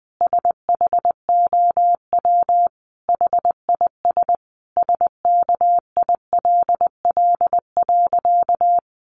Morse code, heard as a keyed tone, about 20 wpm, 700 Hz. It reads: SHOW HIS SKILL.
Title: Sentences from Top 1200 Words 20wpm